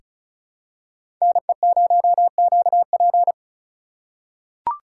Morse code, heard as a keyed tone, about 36 wpm, 700 Hz. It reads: NE0QP
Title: USA Call Signs - 5 Characters - 2x Character Spacing - 35wpm